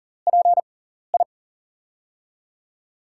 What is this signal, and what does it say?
Morse code, heard as a keyed tone, about 40 wpm, spaced as for 14 wpm, 700 Hz. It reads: PI
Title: USA Call Signs - 4 Characters - 6x Character Spacing - 40wpm